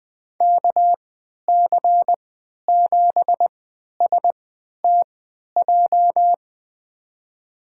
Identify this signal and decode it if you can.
Morse code, heard as a keyed tone, about 20 wpm, spaced as for 11 wpm, 700 Hz. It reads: KC7STJ
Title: USA Call Signs - 6 Characters - 3x Character Spacing - 20wpm